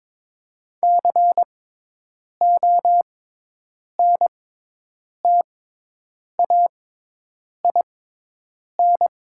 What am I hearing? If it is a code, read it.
Morse code, heard as a keyed tone, about 21 wpm, spaced as for 8 wpm, 700 Hz. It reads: CONTAIN
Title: 6x Character Spacing - Top 500 Words - 22wpm